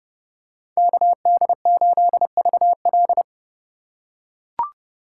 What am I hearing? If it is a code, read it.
Morse code, heard as a keyed tone, about 29 wpm, 700 Hz. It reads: KD8VL